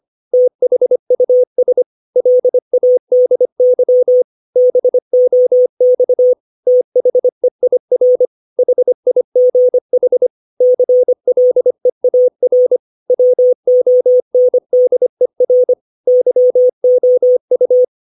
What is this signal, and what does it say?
Morse code, heard as a keyed tone, about 25 wpm, 500 Hz. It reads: THUS LADY BOX THEIR HIGH CLEAR WONDER YOU
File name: 632 words / 632 Words 25WPM